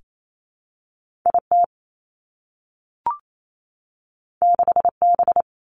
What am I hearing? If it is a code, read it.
Morse code, heard as a keyed tone, about 28 wpm, 700 Hz. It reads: I T 6 B
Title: Two Challenging Letter-Number Combinations 28wpm